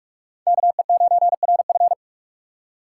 Morse code, heard as a keyed tone, about 46 wpm, 700 Hz. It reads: KE9RF
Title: USA Call Signs - 5 Characters - 2x Character Spacing - 45wpm